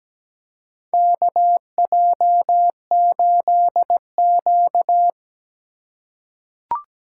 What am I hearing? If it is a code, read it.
Morse code, heard as a keyed tone, about 17 wpm, 700 Hz. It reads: KJ8Q
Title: USA Call Signs - 3x Character Spacing - 17wpm